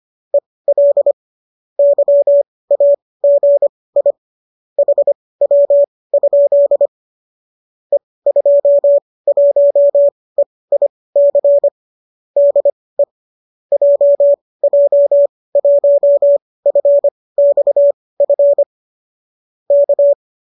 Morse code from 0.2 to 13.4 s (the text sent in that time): EL YAGI HW? E21EIC DE